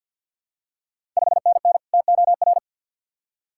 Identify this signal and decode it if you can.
Morse code, heard as a keyed tone, about 51 wpm, 700 Hz. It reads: 5NN TOR
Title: Ontario QSO Party Exchange 50wpm